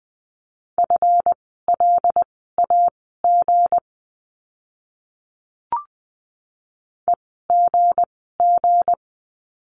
Morse code, heard as a keyed tone, about 20 wpm, spaced as for 14 wpm, 700 Hz. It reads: FLAG EGG
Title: 160 Lesson 26f - Letter g - Words 20wpm